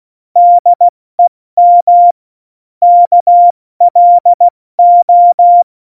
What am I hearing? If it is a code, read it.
Morse code, heard as a keyed tone, about 16 wpm, 700 Hz. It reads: DEM KLO
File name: Morsekurs / Witze / Witz 10. Oktober 17, Tempo 70